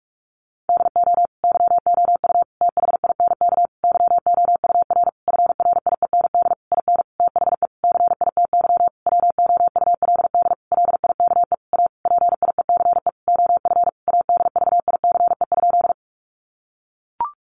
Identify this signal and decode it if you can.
Morse code, heard as a keyed tone, about 44 wpm, 700 Hz. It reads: DO YOU THINK YOUR FRIEND IN THE CITY WOULD LIKE A PIECE OF ADVICE?
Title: Sentences from Top 1100 Words 45wpm